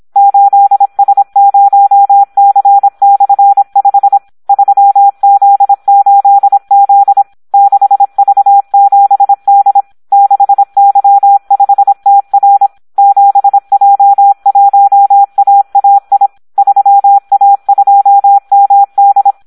Morse code, heard as a keyed tone, about 26 wpm, 800 Hz. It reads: 8S0C/5 3Z8Z 6V7D 6Y5TR 7J1AAI 3A2MD
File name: Audio practica CW / SD 1 26